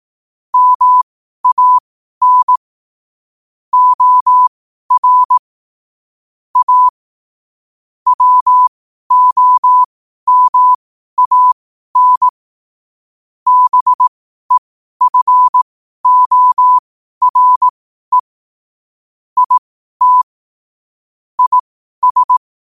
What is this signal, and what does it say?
Morse code, heard as a keyed tone, about 18 wpm, spaced as for 12 wpm, 1000 Hz. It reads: MAN OR A WOMAN BEFORE IT IS
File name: qotdfiles / 12wpm